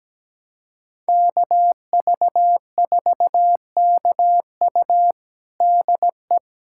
Morse code, heard as a keyed tone, about 17 wpm, 700 Hz. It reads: KV4KU DE
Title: USA Callsign DE Callsign 17wpm